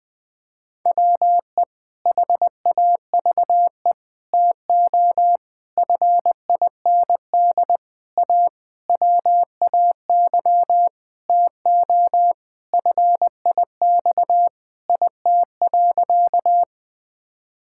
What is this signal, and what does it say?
Morse code, heard as a keyed tone, about 20 wpm, 700 Hz. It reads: WE HAVE TO FIND A WAY TO FIX IT.
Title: Sentences with 7-Word Phrases - Spoken Beginning - 20wpm